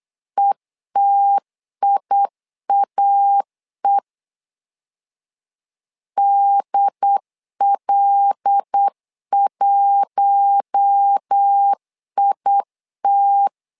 Morse code, heard as a keyed tone, about 8 wpm, 800 Hz. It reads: ETIAE DL1IT
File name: LEZIONI / 5 LBVC / 5 L 8wpm 2831